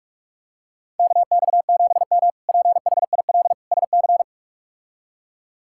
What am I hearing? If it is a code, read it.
Morse code, heard as a keyed tone, about 46 wpm, 700 Hz. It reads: KX7M PHIL SC
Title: K1USN SST QSO Practice - 45wpm